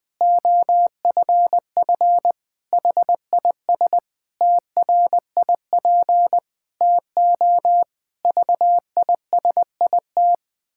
Morse code, heard as a keyed tone, about 20 wpm, 700 Hz. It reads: OFF HIS TRIP TO VISIT
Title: Pangram Challenge v2 - 20wpm